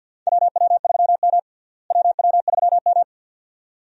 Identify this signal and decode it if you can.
Morse code, heard as a keyed tone, about 52 wpm, 700 Hz. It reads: WW3M WW3M
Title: USA Call Signs - 4 Characters - Repeated 2x - 50wpm